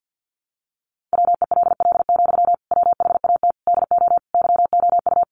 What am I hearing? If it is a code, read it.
Morse code, heard as a keyed tone, about 50 wpm, 700 Hz. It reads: WELL, WHAT DO YOU
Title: Sentences from Top 200 Words 50wpm